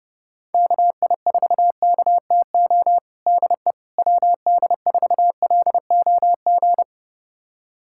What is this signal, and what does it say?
Morse code, heard as a keyed tone, about 30 wpm, 700 Hz. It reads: KI4KTO DE WD4LOG
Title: USA Callsign DE Callsign - 6 Characters - 30wpm